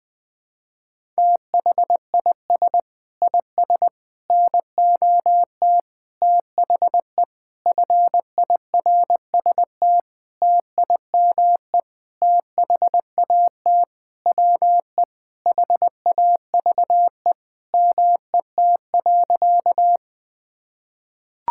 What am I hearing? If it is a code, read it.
Morse code, heard as a keyed tone, about 21 wpm, 700 Hz. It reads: THIS IS NOT THE FIRST TIME THAT WE HAVE MET.
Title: Sentences with 7-Word Phrases - Spoken Beginning - 20wpm